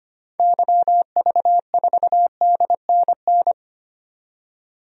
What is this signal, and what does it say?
Morse code, heard as a keyed tone, about 25 wpm, 690 Hz. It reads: YV4DNN
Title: DXCC-100 Call Signs - 6 Characters - 25wpm